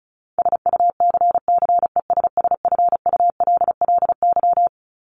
Morse code, heard as a keyed tone, about 34 wpm, 700 Hz. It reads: SUCCESSFULLY
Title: LY Suffix - Top 100 Words with Base Words 35wpm